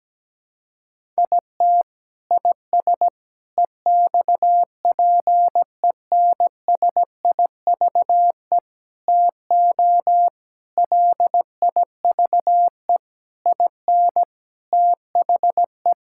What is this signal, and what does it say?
Morse code, heard as a keyed tone, about 17 wpm, 700 Hz. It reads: IT IS EXPENSIVE TO LIVE IN THE
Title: Sentences from Top 1000 Words - Review Part 6 - 17wpm